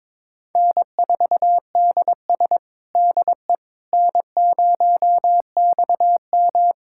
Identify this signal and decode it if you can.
Morse code, heard as a keyed tone, about 22 wpm, 700 Hz. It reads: N4DS DE N0XM
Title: USA Callsign DE Callsign - 4 Characters - 22wpm